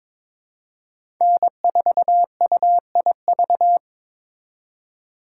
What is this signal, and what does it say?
Morse code, heard as a keyed tone, about 22 wpm, 700 Hz. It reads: N4UIV
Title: USA Call Signs - 3x Character Spacing - 22wpm